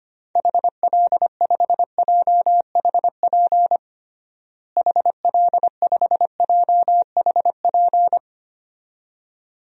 Morse code, heard as a keyed tone, about 25 wpm, 700 Hz. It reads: HL5JHP HL5JHP
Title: DXCC-100 Call Signs - 6 Characters - Repeated 2x - 25wpm